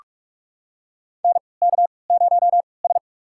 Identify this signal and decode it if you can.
Morse code, heard as a keyed tone, about 42 wpm, spaced as for 25 wpm, 700 Hz. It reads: NK0S
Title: USA Call Signs - 4 Characters - 3x Character Spacing - 45wpm